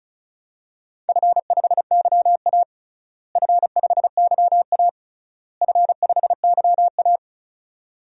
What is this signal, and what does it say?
Morse code, heard as a keyed tone, about 36 wpm, 700 Hz. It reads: F5YA F5YA F5YA
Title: DXCC-100 Call Signs - 4 Characters - Repeated 3x - 35wpm